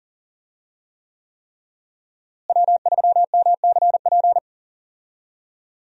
Morse code, heard as a keyed tone, about 40 wpm, 700 Hz. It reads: W3MCP
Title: USA Call Signs - 5 Characters - Repeated 3x - 40wpm